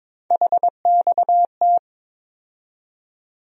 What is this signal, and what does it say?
Morse code, heard as a keyed tone, about 22 wpm, 700 Hz. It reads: HXT
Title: USA Call Sign Postfixes - 3 Letters 22wpm